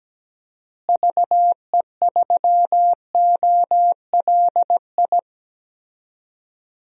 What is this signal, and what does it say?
Morse code, heard as a keyed tone, about 17 wpm, 700 Hz. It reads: VE3OLI
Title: DXCC-100 Call Signs - 6 Characters - Repeated 2x - 17wpm